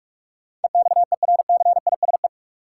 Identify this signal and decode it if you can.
Morse code, heard as a keyed tone, about 42 wpm, 700 Hz. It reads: EXERCISE
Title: Sentences from Top 1200 Words - New Word Drill Alt - 45wpm